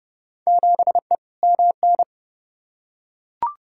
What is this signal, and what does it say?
Morse code, heard as a keyed tone, about 32 wpm, 700 Hz. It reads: 7E MN
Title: Field Day Exchanges Encore - 30wpm